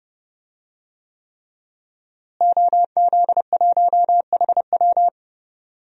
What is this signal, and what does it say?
Morse code, heard as a keyed tone, about 30 wpm, 700 Hz. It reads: OZ1HW
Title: DXCC-100 Call Signs - 5 Characters - Repeated 3x - 30wpm